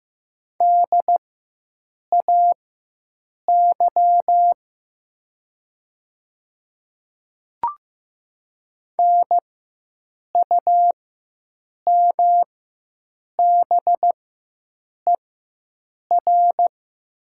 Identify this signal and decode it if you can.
Morse code, heard as a keyed tone, about 15 wpm, spaced as for 7 wpm, 700 Hz. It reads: DAY NUMBER
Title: 4x Character Spacing - Top 100 Words - 15wpm